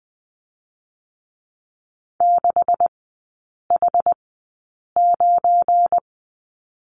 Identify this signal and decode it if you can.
Morse code, heard as a keyed tone, about 20 wpm, spaced as for 8 wpm, 690 Hz. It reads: 6H9
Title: 198 Lesson 32e - Letter k - Three Characters 20wpm